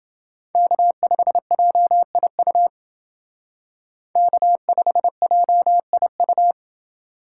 Morse code, heard as a keyed tone, about 28 wpm, 700 Hz. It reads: K5JIU K5JIU